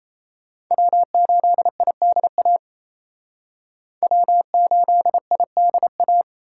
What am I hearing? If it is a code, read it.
Morse code, heard as a keyed tone, about 35 wpm, 700 Hz. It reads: W8IDA W8IDA
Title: Speed Racing - USA 1x3 Call Signs 28wpm